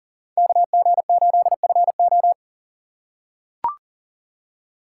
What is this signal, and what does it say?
Morse code, heard as a keyed tone, about 40 wpm, 700 Hz. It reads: KG8FO
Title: USA Call Signs - 4x Character Spacing - 40wpm